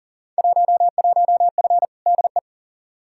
Morse code, heard as a keyed tone, about 40 wpm, 700 Hz. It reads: 11F DE
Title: Field Day Exchanges Encore - 40wpm